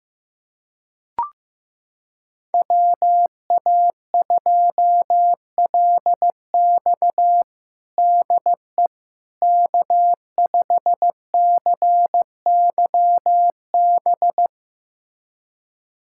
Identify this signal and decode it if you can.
Morse code, heard as a keyed tone, about 15 wpm, 700 Hz. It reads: WA2LX DE K5CYB
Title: USA Callsign DE Callsign - 5 Characters - 15wpm